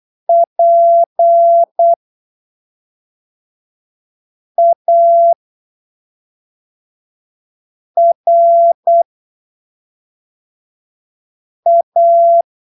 Morse code, heard as a keyed tone, about 8 wpm, spaced as for 3 wpm, 680 Hz. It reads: PARA
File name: arquivos / texto / texto 09 08ppm